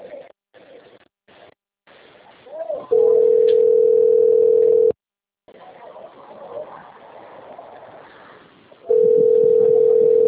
Ringback tone (North America), 440 + 480 Hz.
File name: rg-1600-unknown- - - .17398